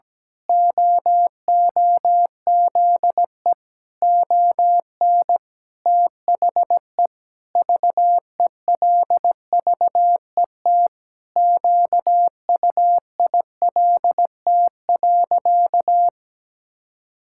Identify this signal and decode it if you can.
Morse code, heard as a keyed tone, about 17 wpm, 700 Hz. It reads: OOZE ON THE VELVET QUILT.
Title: Pangram Challenge v2 - 17wpm